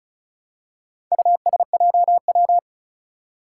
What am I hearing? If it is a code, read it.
Morse code, heard as a keyed tone, about 36 wpm, 700 Hz. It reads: USJW
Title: Four Challenging Letter-Number Combinations 35wpm